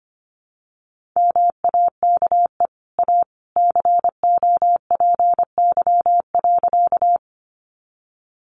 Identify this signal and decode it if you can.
Morse code, heard as a keyed tone, about 25 wpm, 700 Hz. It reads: MAKE A COPY.
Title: Sentences from Top 1400 Words 25wpm